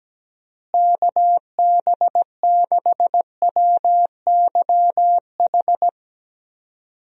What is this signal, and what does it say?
Morse code, heard as a keyed tone, about 17 wpm, 700 Hz. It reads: KB6WYH
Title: USA Call Signs - 6 Characters - Repeated 3x - 17wpm